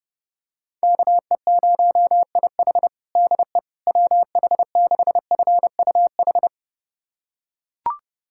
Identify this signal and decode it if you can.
Morse code, heard as a keyed tone, about 30 wpm, 700 Hz. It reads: KE0IH DE WH6FUH
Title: USA Callsign DE Callsign 30wpm